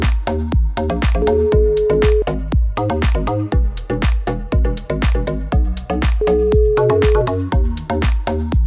Ringback tone (Europe, 425 Hz), with noise.